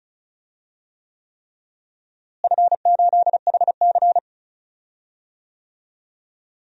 Morse code, heard as a keyed tone, about 36 wpm, 700 Hz. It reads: F8HC
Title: DXCC-100 Call Signs - 4 Characters - Repeated 3x - 35wpm